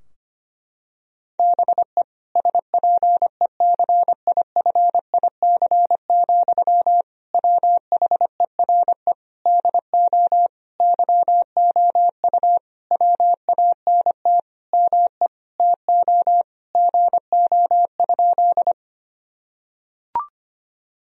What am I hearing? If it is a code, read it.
Morse code, heard as a keyed tone, about 25 wpm, 700 Hz. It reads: BE SPECIFIC, WHERE DO YOU WANT ME TO GO?
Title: Sentences from Top 1000 Words - Review Part 3 - 25wpm